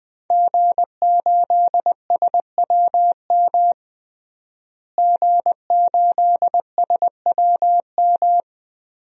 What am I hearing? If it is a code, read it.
Morse code, heard as a keyed tone, about 20 wpm, 700 Hz. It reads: G8SWM G8SWM
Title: DXCC-100 Call Signs - 5 Characters - Repeated 3x - 20wpm